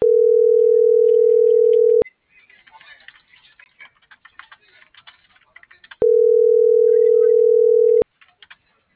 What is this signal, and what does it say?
Ringback tone (North America), 440 + 480 Hz.